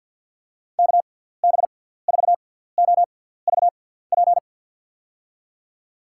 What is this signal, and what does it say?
Morse code, heard as a keyed tone, about 53 wpm, spaced as for 17 wpm, 700 Hz. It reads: KB4OVP
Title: USA Call Signs - 6 Characters - 6x Character Spacing - 50wpm